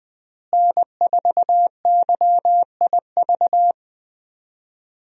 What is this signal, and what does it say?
Morse code, heard as a keyed tone, about 20 wpm, 700 Hz. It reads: N4YIV
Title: USA Call Signs - 5 Characters - Repeated 2x - 20wpm